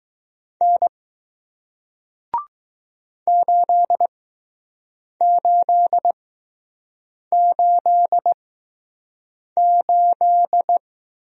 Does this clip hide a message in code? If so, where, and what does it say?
Yes: Morse code from 0.6 to 10.8 s, heard as a keyed tone, about 20 wpm, spaced as for 6 wpm, 700 Hz. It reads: N 8888